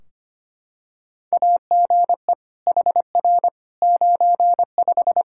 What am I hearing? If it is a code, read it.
Morse code, heard as a keyed tone, about 25 wpm, 700 Hz. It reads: AGE HR 95
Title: Practice QSO - Exchange 3 - More Info - 25wpm